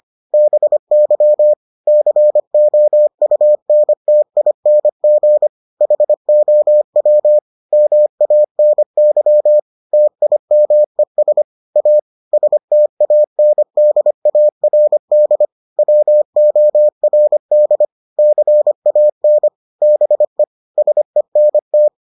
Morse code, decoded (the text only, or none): BY COUNTING HOW MANY TIMES A STANDARD WORD CAN BE SENT